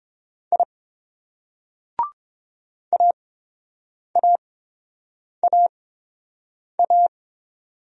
Morse code, heard as a keyed tone, about 27 wpm, spaced as for 7 wpm, 700 Hz. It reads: I AAAA
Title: Speed-Racing Single Letters 22wpm